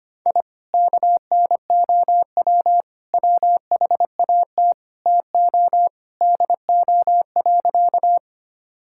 Morse code, heard as a keyed tone, about 25 wpm, 700 Hz. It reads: I KNOW WHAT TO DO.
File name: practice / 25wpm / Sentences with 3-Word Phrases - 25wpm